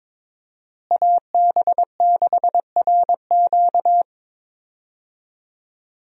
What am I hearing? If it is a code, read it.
Morse code, heard as a keyed tone, about 22 wpm, 700 Hz. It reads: AB6RQ
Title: USA Call Signs - 5 Characters - Repeated 3x - 22wpm